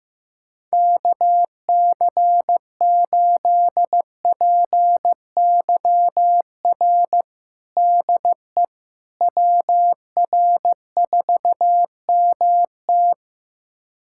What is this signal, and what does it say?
Morse code, heard as a keyed tone, about 15 wpm, 700 Hz. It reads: KC8PYR DE WR4MT
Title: USA Callsign DE Callsign 15wpm